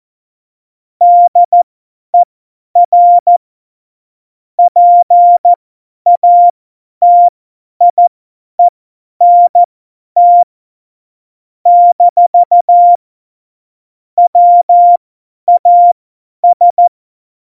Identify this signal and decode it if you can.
Morse code, heard as a keyed tone, about 14 wpm, spaced as for 10 wpm, 700 Hz. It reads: DER PATIENT - WAS
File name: Morsekurs / Witze / Witz 13. Oktober 2020, 50 F70